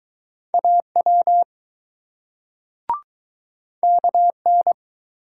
Morse code, heard as a keyed tone, about 21 wpm, 700 Hz. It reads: AW KN